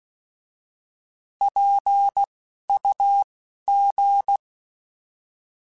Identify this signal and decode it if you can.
Morse code, heard as a keyed tone, about 16 wpm, spaced as for 11 wpm, 800 Hz. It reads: PUG
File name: Alphabetical / Alphabetical mp3s / Ps @10